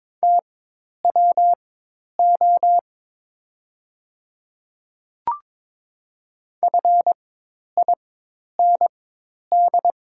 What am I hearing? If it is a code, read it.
Morse code, heard as a keyed tone, about 22 wpm, spaced as for 10 wpm, 700 Hz. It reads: TWO FIND
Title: 4x Character Spacing - Top 200 Words - 22wpm